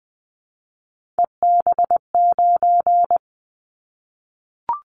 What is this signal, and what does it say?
Morse code, heard as a keyed tone, about 20 wpm, 700 Hz. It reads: EB9
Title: Callsign Prefixes for Top 100 DXCC - 20wpm